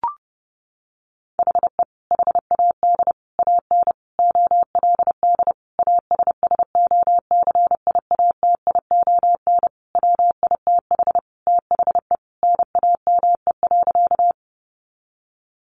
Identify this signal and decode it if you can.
Morse code, heard as a keyed tone, about 30 wpm, 700 Hz. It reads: HE HAD AN OLD ASSOCIATION WITH THE NAME.
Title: Sentences from Top 1100 Words 30wpm